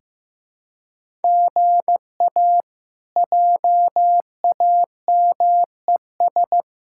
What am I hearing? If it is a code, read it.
Morse code, heard as a keyed tone, about 15 wpm, 700 Hz. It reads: GA JAMES
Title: K1USN SST QSO Practice - 15wpm